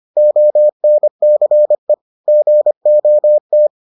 Morse code, heard as a keyed tone, about 25 wpm, 600 Hz. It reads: ONCE GOT